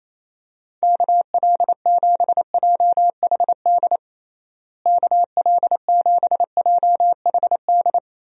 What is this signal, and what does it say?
Morse code, heard as a keyed tone, about 28 wpm, 700 Hz. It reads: KL7JHD KL7JHD
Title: USA Call Signs - 6 Characters - Repeated 3x - 28wpm